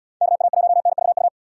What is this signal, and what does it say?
Morse code, heard as a keyed tone, about 68 wpm, 700 Hz. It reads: KA1NVU